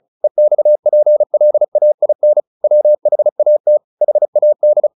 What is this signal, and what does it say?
Morse code, heard as a keyed tone, about 35 wpm, 600 Hz. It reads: EXPLAIN WHAT HAD
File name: the old man and the sea 2 35wpm